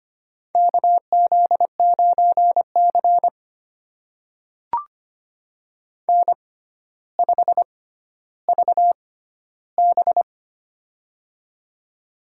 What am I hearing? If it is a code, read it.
Morse code, heard as a keyed tone, about 25 wpm, spaced as for 13 wpm, 700 Hz. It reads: KZ9C N5VB
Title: USA Call Signs - 4 Characters - 6x Character Spacing - 25wpm